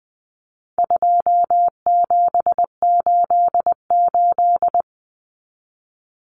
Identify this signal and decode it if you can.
Morse code, heard as a keyed tone, about 20 wpm, 700 Hz. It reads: 2788
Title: Common Cut Numbers - 4-Digit 20wpm